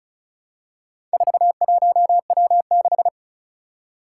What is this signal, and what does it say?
Morse code, heard as a keyed tone, about 36 wpm, 700 Hz. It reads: 41W6
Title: Four Challenging Letter-Number Combinations 35wpm